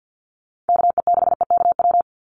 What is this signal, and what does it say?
Morse code, heard as a keyed tone, about 48 wpm, 700 Hz. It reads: KE6EKW
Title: USA Call Signs 50wpm